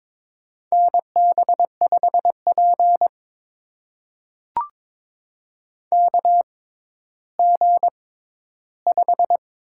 Morse code, heard as a keyed tone, about 22 wpm, spaced as for 12 wpm, 700 Hz. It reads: NB5P KG5
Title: USA Call Signs - 6x Character Spacing - 22wpm